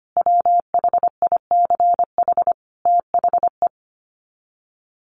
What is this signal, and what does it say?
Morse code, heard as a keyed tone, about 25 wpm, 700 Hz. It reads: WHICH THE
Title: Sets of 5 Words Encore2 25wpm